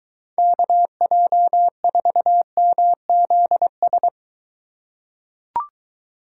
Morse code, heard as a keyed tone, about 21 wpm, 700 Hz. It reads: KJ4MZS